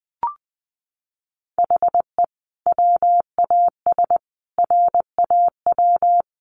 Morse code, heard as a keyed tone, about 20 wpm, 700 Hz. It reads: HE WAS RAW
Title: Sentences from Top 1500 Words 20wpm